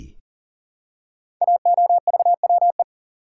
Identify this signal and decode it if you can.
Morse code, heard as a keyed tone, about 42 wpm, 690 Hz. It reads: AOVWE
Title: Code Groups - 5 Letters - 40wpm